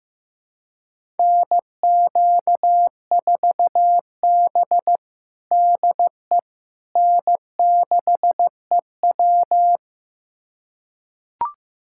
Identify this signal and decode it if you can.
Morse code, heard as a keyed tone, about 15 wpm, 700 Hz. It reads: NQ4B DE N6EW
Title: USA Callsign DE Callsign - 4 Characters - 15wpm